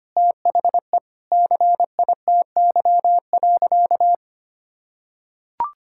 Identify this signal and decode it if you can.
Morse code, heard as a keyed tone, about 25 wpm, 700 Hz. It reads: THE CITY.
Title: Sentences from Top 1000 Words - Review Part 6 - 25wpm